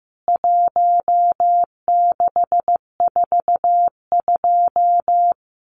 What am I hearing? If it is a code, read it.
Morse code, heard as a keyed tone, about 15 wpm, 700 Hz. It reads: 1642
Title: CWT Exchange 15wpm